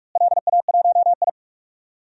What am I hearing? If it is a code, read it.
Morse code, heard as a keyed tone, about 43 wpm, 700 Hz. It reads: LA1I
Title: DXCC-100 Call Signs - 4 Characters - Repeated 2x - 45wpm